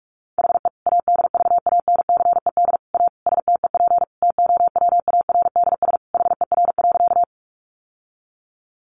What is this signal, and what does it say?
Morse code, heard as a keyed tone, about 44 wpm, 700 Hz. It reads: HE ADVANCED A STEP TOWARDS HER.
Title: Sentences from Top 1200 Words 45wpm